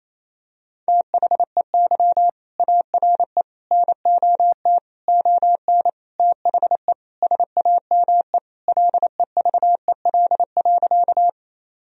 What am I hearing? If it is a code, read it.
Morse code, heard as a keyed tone, about 28 wpm, 700 Hz. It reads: THEY ARE NOT ON THE SAME LEVEL.
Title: Sentences from Top 1000 Words - Review Part 4 - 28wpm